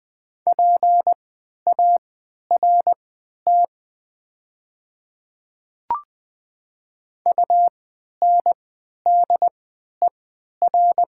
Morse code, heard as a keyed tone, about 20 wpm, spaced as for 11 wpm, 700 Hz. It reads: PART UNDER
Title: 3x Character Spacing - Top 300 Words - 20wpm